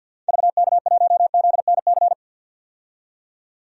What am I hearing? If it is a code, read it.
Morse code, heard as a keyed tone, about 52 wpm, 700 Hz. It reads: VK1ZNC
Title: DXCC-100 Call Signs - 6 Characters - 50wpm